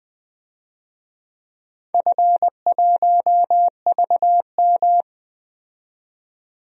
Morse code, heard as a keyed tone, about 20 wpm, 700 Hz. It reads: F1VM
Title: DXCC-100 Call Signs - 4 Characters - 20wpm